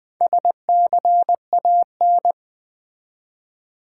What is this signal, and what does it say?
Morse code, heard as a keyed tone, about 20 wpm, 700 Hz. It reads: SCAN
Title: Build A Word - 1 Letter At A Time - 4-Letter Words - 20wpm